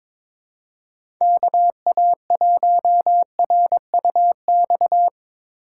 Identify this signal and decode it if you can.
Morse code, heard as a keyed tone, about 22 wpm, 700 Hz. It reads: KA1RUX
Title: USA Call Signs - 2x Character Spacing - 22wpm